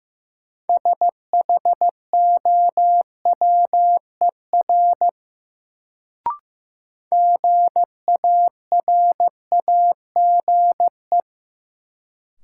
Morse code, heard as a keyed tone, about 15 wpm, 700 Hz. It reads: SHOWER GARAGE
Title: Sentences from Top 1600 Words - New Word Drill Alt - 15wpm